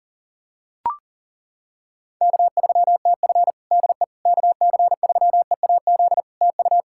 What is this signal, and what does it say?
Morse code, heard as a keyed tone, about 42 wpm, 700 Hz. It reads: K3TF DE KC3EAZ TU
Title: Practice QSO - Exchange 4 - Ending - 40wpm